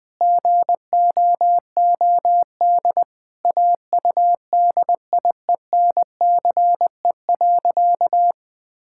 Morse code, heard as a keyed tone, about 20 wpm, 700 Hz. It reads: GOOD AUDIENCE.
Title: Sentences from Top 1000 Words - Review Part 5 - 20wpm